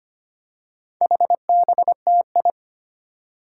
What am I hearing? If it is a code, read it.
Morse code, heard as a keyed tone, about 25 wpm, 700 Hz. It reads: HBTI
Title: Four Challenging Letter-Number Combinations 25wpm